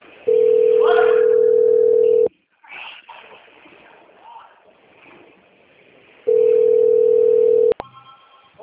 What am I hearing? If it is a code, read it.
Ringback tone (North America), 440 + 480 Hz.